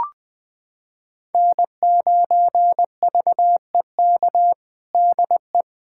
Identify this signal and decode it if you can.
Morse code, heard as a keyed tone, about 20 wpm, 700 Hz. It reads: N9VEK DE
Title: USA Callsign DE Callsign 20wpm